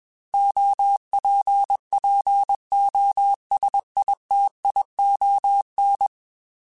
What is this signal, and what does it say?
Morse code, heard as a keyed tone, about 21 wpm, 800 Hz. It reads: OPPOSITION